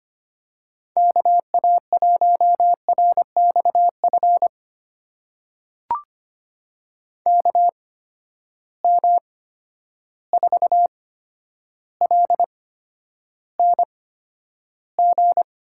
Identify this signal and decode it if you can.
Morse code, heard as a keyed tone, about 25 wpm, spaced as for 11 wpm, 700 Hz. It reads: KA1RXF KM4LNG
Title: USA Call Signs - 6 Characters - 8x Character Spacing - 25wpm